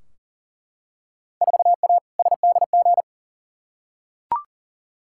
Morse code, heard as a keyed tone, about 40 wpm, 700 Hz. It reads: 4A SDG
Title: Field Day Exchanges Encore - 40wpm